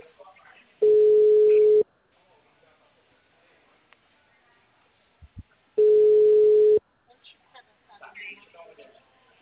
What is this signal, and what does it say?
Ringback tone (Europe, 425 Hz), with speech.